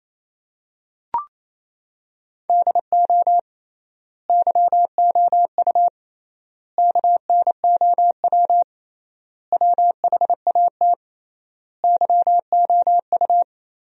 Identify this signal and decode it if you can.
Morse code, heard as a keyed tone, about 28 wpm, 700 Hz. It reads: DO YOU KNOW WHAT YOU
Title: 3x Word-Spacing - Sentences from Top 100 Words - 28wpm